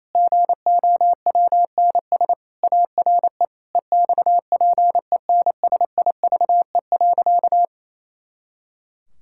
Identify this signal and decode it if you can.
Morse code, heard as a keyed tone, about 28 wpm, 700 Hz. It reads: GOWNS ARE EXPENSIVE.
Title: Pangram Challenge v2 - 28wpm